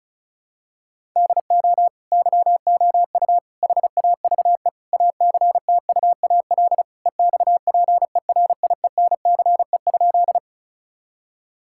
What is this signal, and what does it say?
Morse code, heard as a keyed tone, about 34 wpm, 700 Hz. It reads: DO YOU HAVE ACTUAL EXPERIENCE?
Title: Sentences from Top 1000 Words - Review Part 5 - 35wpm